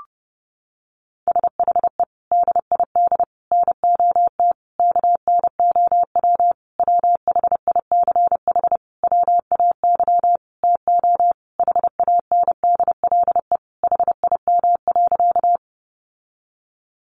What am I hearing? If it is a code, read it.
Morse code, heard as a keyed tone, about 30 wpm, 700 Hz. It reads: SHE DID NOT KNOW WHICH WAY TO HANDLE HIM.
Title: Sentences from Top 1100 Words 30wpm